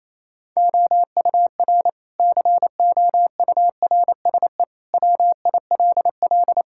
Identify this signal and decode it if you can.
Morse code, heard as a keyed tone, about 28 wpm, 700 Hz. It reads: OUR COURSE WILL
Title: Sentences from Top 1000 Words - Review Part 6 - 28wpm